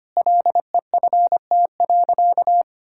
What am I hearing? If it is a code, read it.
Morse code, heard as a keyed tone, about 25 wpm, 700 Hz. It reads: LEFT.
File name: practice / 25wpm / Sentences with 3-Word Phrases - 25wpm